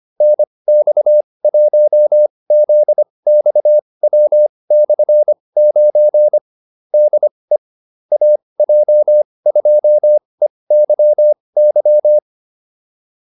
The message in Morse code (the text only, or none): NX1ZXW/9 DE AJ2EYY